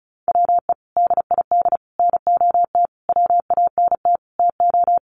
Morse code, heard as a keyed tone, about 34 wpm, 700 Hz. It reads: WE DID NOT WANT TO
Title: Sets of 5 Words Encore 35wpm